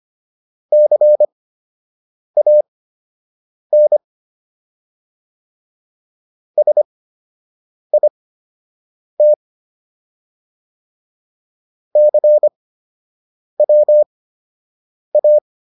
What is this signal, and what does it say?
Morse code, heard as a keyed tone, about 25 wpm, spaced as for 7 wpm, 600 Hz. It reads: CAN SIT CWA